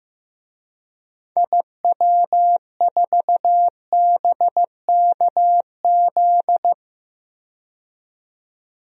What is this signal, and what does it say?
Morse code, heard as a keyed tone, about 15 wpm, 700 Hz. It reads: IW4BKZ
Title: DXCC-100 Call Signs - 6 Characters - 15wpm